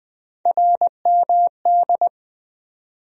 Morse code, heard as a keyed tone, about 20 wpm, 700 Hz. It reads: RMD
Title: USA Call Sign Postfixes - 3 Letters 20wpm